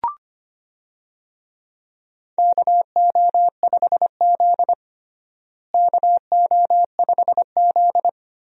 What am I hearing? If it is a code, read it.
Morse code, heard as a keyed tone, about 25 wpm, 700 Hz. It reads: KO5Z KO5Z
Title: USA Call Signs - 4 Characters - Repeated 3x - 25wpm